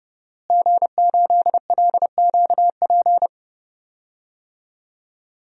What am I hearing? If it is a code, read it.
Morse code, heard as a keyed tone, about 32 wpm, 700 Hz. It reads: G8LQP
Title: DXCC-100 Call Signs - 5 Characters - Repeated 2x - 30wpm